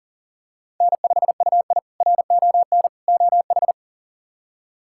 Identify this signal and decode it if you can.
Morse code, heard as a keyed tone, about 38 wpm, 700 Hz. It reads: N5UI RON OH
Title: K1USN SST QSO Practice - 40wpm